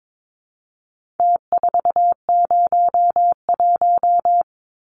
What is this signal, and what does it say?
Morse code, heard as a keyed tone, about 22 wpm, 700 Hz. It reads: T401
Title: Numbers - Contest Style - 4-Digit 22wpm